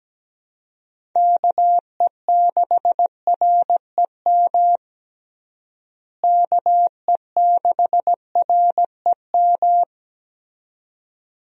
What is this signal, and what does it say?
Morse code, heard as a keyed tone, about 17 wpm, 700 Hz. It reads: KE6REM KE6REM
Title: USA Call Signs - 6 Characters - Repeated 2x - 17wpm